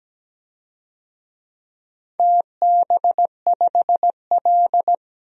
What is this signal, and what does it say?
Morse code, heard as a keyed tone, about 17 wpm, 700 Hz. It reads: TB5L
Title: DXCC-100 Call Signs - 4 Characters - 17wpm